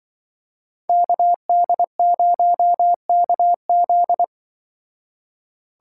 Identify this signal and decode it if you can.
Morse code, heard as a keyed tone, about 24 wpm, 700 Hz. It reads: KD0KZ
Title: Speed Racing - USA 2x2 Call Signs 20wpm